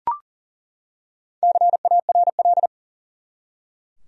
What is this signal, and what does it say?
Morse code, heard as a keyed tone, about 40 wpm, 700 Hz. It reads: CARL
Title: USA Common Names - 40wpm